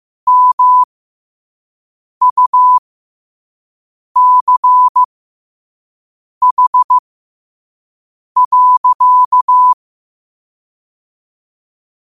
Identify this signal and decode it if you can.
Morse code, heard as a keyed tone, about 15 wpm, spaced as for 5 wpm, 1000 Hz. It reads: MUCH.
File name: qotdfiles / 5wpm